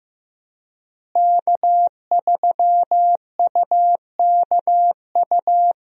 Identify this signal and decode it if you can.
Morse code, heard as a keyed tone, about 15 wpm, 700 Hz. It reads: K3UKU
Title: USA Call Signs - 5 Characters - Repeated 3x - 15wpm